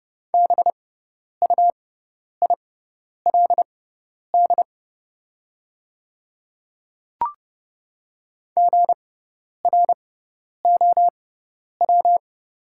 Morse code, heard as a keyed tone, about 30 wpm, spaced as for 10 wpm, 700 Hz. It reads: BUILD GROW
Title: 6x Character Spacing - Top 300 Words - 30wpm